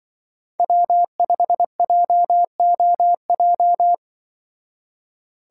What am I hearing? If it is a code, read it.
Morse code, heard as a keyed tone, about 24 wpm, 700 Hz. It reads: W5JOJ
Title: Speed Racing - USA 1x3 Call Signs 20wpm